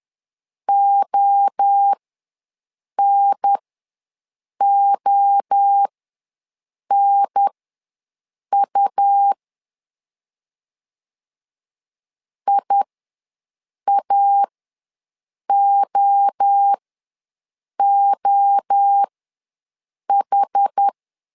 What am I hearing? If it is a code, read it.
Morse code, heard as a keyed tone, about 10 wpm, spaced as for 6 wpm, 800 Hz. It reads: ONONU IAOOH
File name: LEZIONI / 3 MWRO / 3 MWRO ws 10wpm 2310